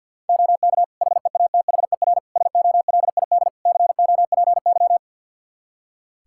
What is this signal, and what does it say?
Morse code, heard as a keyed tone, about 51 wpm, 700 Hz. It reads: OK HEATHER SOLID COPY
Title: K1USN SST QSO Practice 50wpm